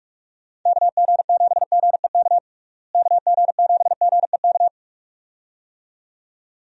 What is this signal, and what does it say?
Morse code, heard as a keyed tone, about 45 wpm, 700 Hz. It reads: KG7GEK KG7GEK
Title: USA Call Signs - 6 Characters - Repeated 2x - 45wpm